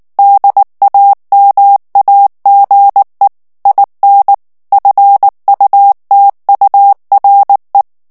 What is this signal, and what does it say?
Morse code, heard as a keyed tone, about 19 wpm, 800 Hz. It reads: DAMAGE IN FUTURE
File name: morse mp3 / ANGLAIS texte clair 2 vitesse 18 mots minute